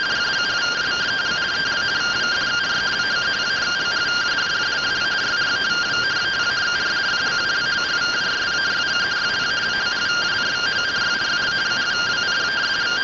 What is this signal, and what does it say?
Two-tone teleprinter signal, about 170 Hz shift.